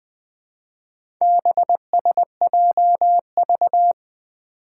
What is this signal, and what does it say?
Morse code, heard as a keyed tone, about 20 wpm, 700 Hz. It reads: BSJV
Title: Four Challenging Letter-Number Combinations 20wpm